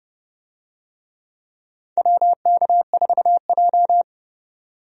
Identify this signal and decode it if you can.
Morse code, heard as a keyed tone, about 30 wpm, 700 Hz. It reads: WK4J
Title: USA Call Signs - 4 Characters - 30wpm